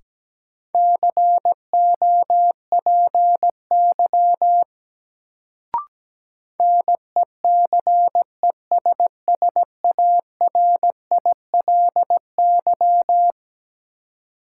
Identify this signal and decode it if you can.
Morse code, heard as a keyed tone, about 17 wpm, 700 Hz. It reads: COPY NECESSARILY
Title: Sentences from Top 1400 Words - New Word Drill Alt - 17wpm